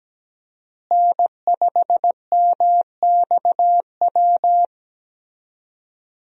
Morse code, heard as a keyed tone, about 17 wpm, 700 Hz. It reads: N5MXW
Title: USA Call Signs - 5 Characters - Repeated 3x - 17wpm